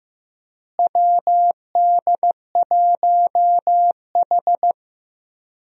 Morse code, heard as a keyed tone, about 16 wpm, 700 Hz. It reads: WD1H
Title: USA Call Signs - 4 Characters - Repeated 2x - 15wpm